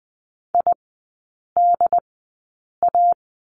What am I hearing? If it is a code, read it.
Morse code, heard as a keyed tone, about 19 wpm, spaced as for 8 wpm, 700 Hz. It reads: IDA
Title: 100 Lesson 17e - Letter u - Three Characters 20wpm